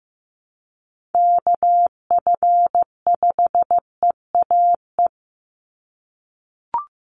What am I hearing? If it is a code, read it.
Morse code, heard as a keyed tone, about 15 wpm, 700 Hz. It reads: KF5EAE
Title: USA Call Signs 15wpm